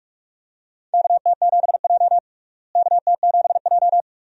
Morse code, heard as a keyed tone, about 46 wpm, 700 Hz. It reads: KT7J KT7J
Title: USA Call Signs - 4 Characters - Repeated 2x - 45wpm